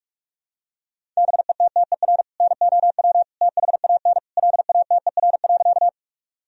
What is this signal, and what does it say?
Morse code, heard as a keyed tone, about 48 wpm, 700 Hz. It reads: BETTER NOW THAN LATER.
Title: Sentences from Top 1000 Words - Review Part 4 - 45wpm